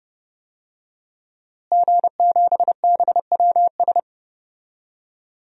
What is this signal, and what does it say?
Morse code, heard as a keyed tone, about 30 wpm, 700 Hz. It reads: G7BWS
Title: DXCC-100 Call Signs - 5 Characters - Repeated 3x - 30wpm